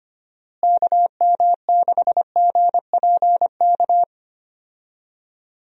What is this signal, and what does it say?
Morse code, heard as a keyed tone, about 25 wpm, 700 Hz. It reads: KM6GPK
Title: USA Call Signs - 6 Characters - Repeated 2x - 25wpm